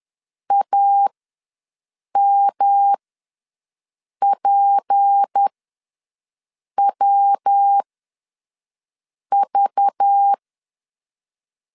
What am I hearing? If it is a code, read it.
Morse code, heard as a keyed tone, about 10 wpm, spaced as for 5 wpm, 800 Hz. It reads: AMPWV
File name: LEZIONI / 7 GF9Z / 7 GF ws 10wpm 3918